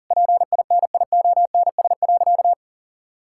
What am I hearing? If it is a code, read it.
Morse code, heard as a keyed tone, about 37 wpm, 700 Hz. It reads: PINIONS.
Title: Pangram Challenge v2 - 40wpm